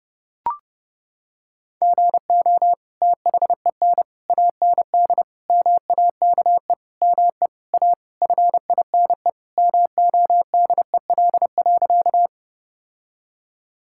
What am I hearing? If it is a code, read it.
Morse code, heard as a keyed tone, about 29 wpm, 700 Hz. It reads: GO THEN AND MAKE ME A FINE MODEL.
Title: Sentences from Top 1000 Words - Review Part 5 - 30wpm